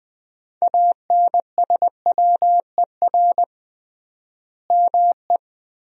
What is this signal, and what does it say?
Morse code, heard as a keyed tone, about 21 wpm, 700 Hz. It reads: ANSWER ME
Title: 3x Word-Spacing - Sentences from Top 500 Words - 20wpm